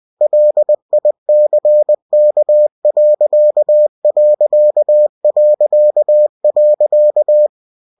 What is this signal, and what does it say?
Morse code, heard as a keyed tone, about 20 wpm, 600 Hz. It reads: LICK....
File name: cw-books / winnie-the-pooh / winnie-20wpm-page-0040